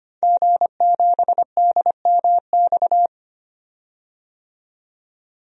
Morse code, heard as a keyed tone, about 25 wpm, 700 Hz. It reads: G7DMX
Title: DXCC-100 Call Signs - 5 Characters - 25wpm